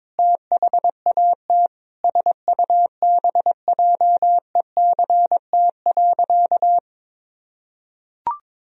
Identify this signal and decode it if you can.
Morse code, heard as a keyed tone, about 22 wpm, 700 Hz. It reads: THAT SUBJECT.
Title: Sentences from Top 1000 Words - Review Part 1 - 22wpm